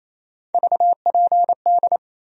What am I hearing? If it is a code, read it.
Morse code, heard as a keyed tone, about 28 wpm, 700 Hz. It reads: VPD
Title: USA Call Sign Postfixes - 3 Letters 28wpm